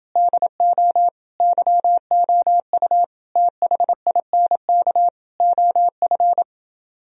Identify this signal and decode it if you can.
Morse code, heard as a keyed tone, about 29 wpm, 700 Hz. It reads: DO YOU THINK OF